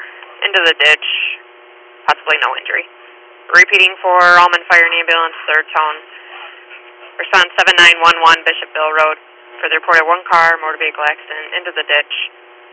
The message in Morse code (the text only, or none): none